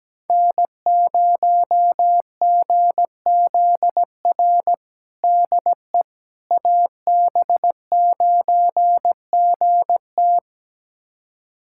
Morse code, heard as a keyed tone, about 17 wpm, 700 Hz. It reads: N0GZR DE AB9GT
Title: USA Callsign DE Callsign - 5 Characters - 17wpm